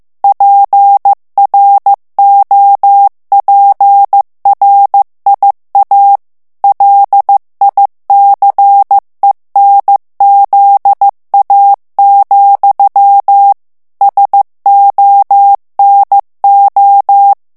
Morse code, heard as a keyed tone, about 15 wpm, 800 Hz. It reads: PROPRIA LICENZA, SONO
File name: morse mp3 / ITALIEN texte clair 2 vitesse 14 mots minute